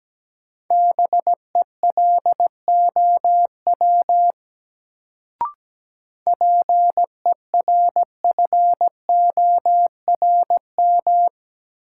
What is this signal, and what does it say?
Morse code, heard as a keyed tone, about 17 wpm, 700 Hz. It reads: BELOW PERFORM
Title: Sentences from Top 1000 Words - Review Part 5 - New Word Drill - 17wpm